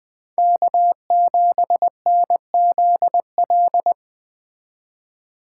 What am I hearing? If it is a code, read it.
Morse code, heard as a keyed tone, about 21 wpm, 700 Hz. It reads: K7NZL